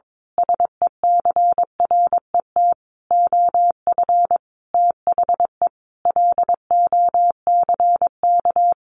Morse code, heard as a keyed tone, about 22 wpm, 700 Hz. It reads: SECRET OF THE LOCK
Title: Sentences from Top 1300 Words 22wpm